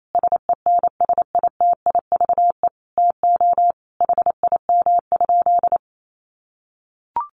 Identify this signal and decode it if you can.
Morse code, heard as a keyed tone, about 28 wpm, 700 Hz. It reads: SENSITIVE TO HIM?
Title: Sentences from Top 1400 Words 28wpm